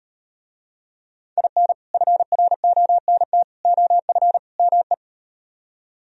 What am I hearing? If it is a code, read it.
Morse code, heard as a keyed tone, about 38 wpm, 700 Hz. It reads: IN FRONT OF ME